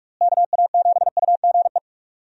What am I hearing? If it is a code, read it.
Morse code, heard as a keyed tone, about 46 wpm, 700 Hz. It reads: KA7UGE
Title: USA Call Signs - 6 Characters - Repeated 3x - 45wpm